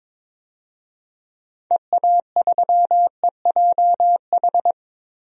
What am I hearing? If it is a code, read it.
Morse code, heard as a keyed tone, about 21 wpm, 700 Hz. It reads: EA3EJH
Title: DXCC-100 Call Signs - 6 Characters - 22wpm